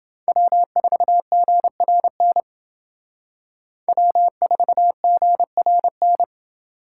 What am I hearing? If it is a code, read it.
Morse code, heard as a keyed tone, about 30 wpm, 700 Hz. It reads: W4GRN W4GRN